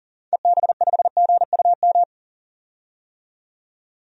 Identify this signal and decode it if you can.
Morse code, heard as a keyed tone, about 40 wpm, 700 Hz. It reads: EB5GUM
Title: DXCC-100 Call Signs - 6 Characters - 40wpm